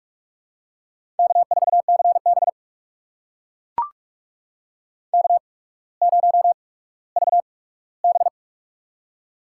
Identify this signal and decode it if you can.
Morse code, heard as a keyed tone, about 44 wpm, spaced as for 19 wpm, 700 Hz. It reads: K4CB K0VB
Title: USA Call Signs - 4 Characters - 8x Character Spacing - 45wpm